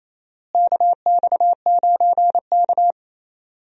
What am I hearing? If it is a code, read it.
Morse code, heard as a keyed tone, about 28 wpm, 700 Hz. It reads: KX9K
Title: USA Call Signs - 4 Characters - Repeated 2x - 28wpm